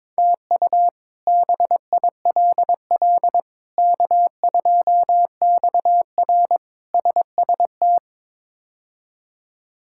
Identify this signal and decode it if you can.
Morse code, heard as a keyed tone, about 22 wpm, 700 Hz. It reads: TU BILL K2XR SST
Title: K1USN SST QSO Practice - 22wpm